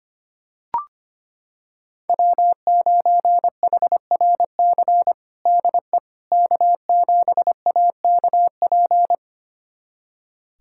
Morse code, heard as a keyed tone, about 25 wpm, 700 Hz. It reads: W9HRC DE K7AKP
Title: USA Callsign DE Callsign - 5 Characters - 25wpm